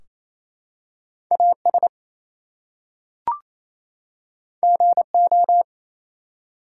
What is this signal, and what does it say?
Morse code, heard as a keyed tone, about 28 wpm, 700 Hz. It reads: AS GO
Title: Two Letter Words - 28wpm